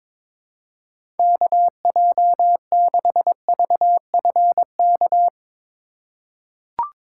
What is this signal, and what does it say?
Morse code, heard as a keyed tone, about 21 wpm, 700 Hz. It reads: KJ6VFK
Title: USA Call Signs - 6 Characters - 3x Character Spacing - 22wpm